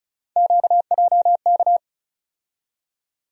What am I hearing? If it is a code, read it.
Morse code, heard as a keyed tone, about 36 wpm, 700 Hz. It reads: QJK
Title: USA Call Sign Postfixes - 3 Letters 35wpm